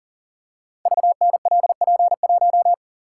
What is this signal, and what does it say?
Morse code, heard as a keyed tone, about 40 wpm, 690 Hz. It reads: VNLP1
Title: Code Groups - 5 Letter:Numbers - 40wpm